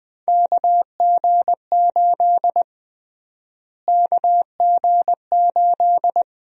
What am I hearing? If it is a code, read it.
Morse code, heard as a keyed tone, about 20 wpm, 700 Hz. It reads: KG8 KG8
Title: USA Call Sign Prefixes - Repeated 2x - 20wpm